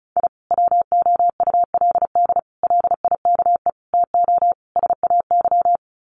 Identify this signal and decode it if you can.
Morse code, heard as a keyed tone, about 34 wpm, 700 Hz. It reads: I WOULD LIKE TO SAY
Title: Sets of 5 Words Encore2 35wpm